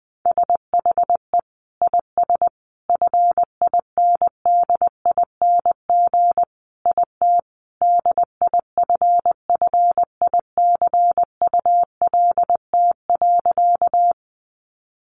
Morse code, heard as a keyed tone, about 20 wpm, 700 Hz. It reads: SHE IS FINDING IT DIFFICULT.
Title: Sentences from Top 1100 Words 20wpm